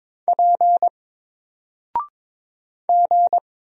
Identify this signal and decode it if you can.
Morse code, heard as a keyed tone, about 22 wpm, 700 Hz. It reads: P G
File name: Single-Letter-Number-22wpm